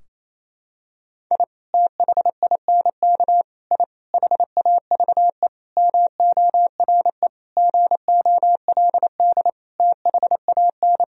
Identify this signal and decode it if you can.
Morse code, heard as a keyed tone, about 27 wpm, 700 Hz. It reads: I THINK I HAVE MORE GOLD THAN
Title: Sentences from Top 1000 Words - Review Part 3 - 28wpm